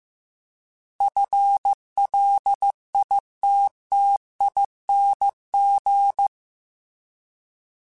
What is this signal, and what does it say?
Morse code, heard as a keyed tone, about 15 wpm, 800 Hz. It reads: FLITTING
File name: FL / FL mp3s / FL Prefixes 3x2 sp @13